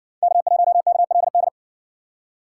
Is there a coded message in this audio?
Yes: Morse code from 0.2 to 1.5 s, heard as a keyed tone, about 56 wpm, 700 Hz. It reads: K1KLD